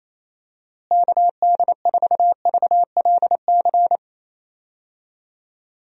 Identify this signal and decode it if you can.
Morse code, heard as a keyed tone, about 28 wpm, 700 Hz. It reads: KD4VLC
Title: USA Call Signs - 6 Characters - Repeated 2x - 28wpm